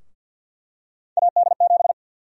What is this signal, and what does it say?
Morse code, heard as a keyed tone, about 50 wpm, 700 Hz. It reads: AD7
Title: USA Call Sign Prefixes - 50wpm